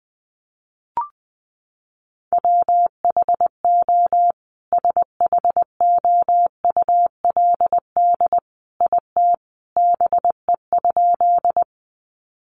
Morse code, heard as a keyed tone, about 20 wpm, 700 Hz. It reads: WHO SHOULD IT BE?
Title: Sentences from Top 300 Words 20wpm